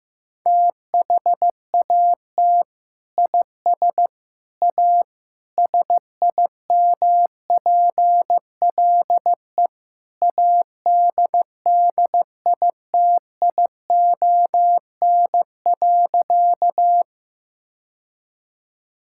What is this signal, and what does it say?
Morse code, heard as a keyed tone, about 15 wpm, 700 Hz. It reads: THAT IS A SIMPLE ADDITION.
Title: Sentences from Top 1000 Words - Review Part 6 - 15wpm